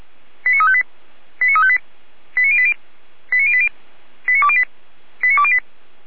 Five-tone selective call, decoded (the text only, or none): 89358 89358 89080 89080 89208 89208